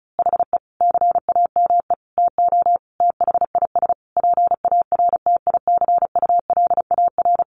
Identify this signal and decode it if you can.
Morse code, heard as a keyed tone, about 34 wpm, 700 Hz. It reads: HE CAME TO THIS PARTICULAR
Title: A Sound Of Thunder 35wpm